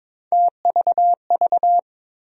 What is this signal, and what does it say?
Morse code, heard as a keyed tone, about 22 wpm, 700 Hz. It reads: TVV
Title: USA Call Sign Postfixes - 3 Letters 22wpm